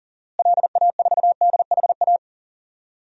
Morse code, heard as a keyed tone, about 40 wpm, 700 Hz. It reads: LA4DHA
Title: DXCC-100 Call Signs - 6 Characters - Repeated 3x - 40wpm